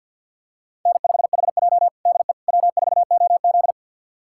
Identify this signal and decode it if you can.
Morse code, heard as a keyed tone, about 47 wpm, 700 Hz. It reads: N5HJ DE W4OZ
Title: USA Callsign DE Callsign - 4 Characters - 50wpm